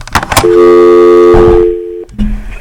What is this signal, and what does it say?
Dial tone (North America), 350 + 440 Hz.